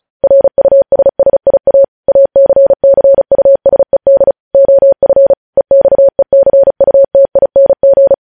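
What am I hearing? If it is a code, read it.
Morse code, heard as a keyed tone, about 36 wpm, 550 Hz. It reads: RUSSIA ACCUSED OF EXECUTING
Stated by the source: Morse code news headlines sent at 35.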